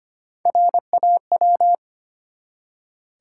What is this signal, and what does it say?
Morse code, heard as a keyed tone, about 25 wpm, 700 Hz. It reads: RAW
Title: Three Letter Words - 25wpm